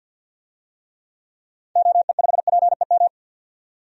Morse code, heard as a keyed tone, about 50 wpm, 700 Hz. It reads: OE5PEM
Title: DXCC-100 Call Signs - 6 Characters - 50wpm